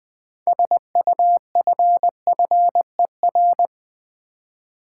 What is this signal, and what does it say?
Morse code, heard as a keyed tone, about 20 wpm, 700 Hz. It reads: SUFFER
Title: 6 Letter Words - First 3 Letters Given - 20wpm